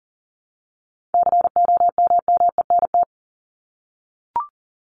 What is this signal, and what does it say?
Morse code, heard as a keyed tone, about 40 wpm, 700 Hz. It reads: COMMENT
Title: Top 1751-2000 Words 40wpm